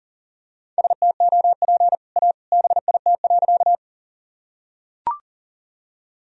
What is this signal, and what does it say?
Morse code, heard as a keyed tone, about 40 wpm, 700 Hz. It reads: STOP A BIT.
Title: Sentences from Top 1000 Words - Review Part 1 - 40wpm